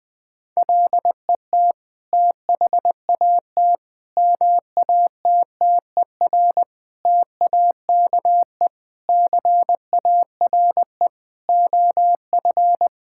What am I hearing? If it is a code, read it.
Morse code, heard as a keyed tone, about 21 wpm, 700 Hz. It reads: LET THAT MATTER TAKE CARE OF
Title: Sentences from Top 1000 Words - Review Part 5 - 20wpm